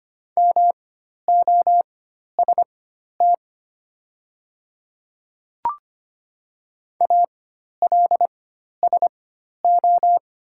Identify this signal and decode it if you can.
Morse code, heard as a keyed tone, about 25 wpm, spaced as for 12 wpm, 700 Hz. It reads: MOST ALSO
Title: 4x Character Spacing - Top 200 Words - 25wpm